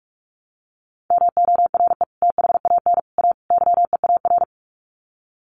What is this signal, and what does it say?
Morse code, heard as a keyed tone, about 44 wpm, 700 Hz. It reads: MORE THAN A YEAR
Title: Sets of 4 Words 45wpm